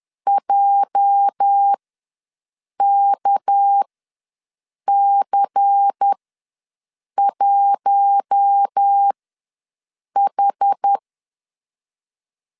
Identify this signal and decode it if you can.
Morse code, heard as a keyed tone, about 10 wpm, spaced as for 6 wpm, 800 Hz. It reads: JKC1H
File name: LEZIONI / 8 YJ45 / 8 YJ4 ws 10wpm 4020